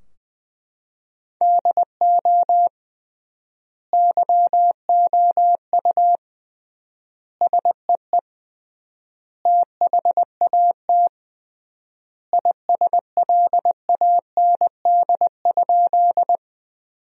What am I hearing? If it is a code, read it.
Morse code, heard as a keyed tone, about 20 wpm, 700 Hz. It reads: DO YOU SEE THAT ISLAND?
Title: 3x Word-Spacing - Sentences from Top 500 Words - 20wpm